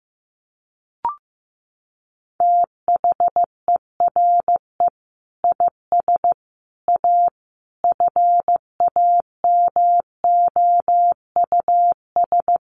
Morse code, heard as a keyed tone, about 15 wpm, 700 Hz. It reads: THERE IS A FAMOUS
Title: Sentences from Top 1500 Words 15wpm